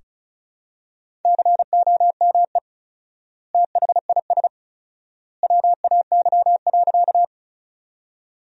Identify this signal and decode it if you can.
Morse code, heard as a keyed tone, about 34 wpm, 700 Hz. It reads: COME THIS WAY.
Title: 4x Word-Spacing - Sentences from Top 100 Words - 35wpm